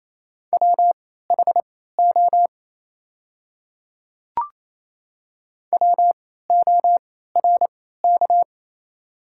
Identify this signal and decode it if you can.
Morse code, heard as a keyed tone, about 27 wpm, spaced as for 16 wpm, 700 Hz. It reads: WHO WORK
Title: 3x Character Spacing - Top 500 Words - 28wpm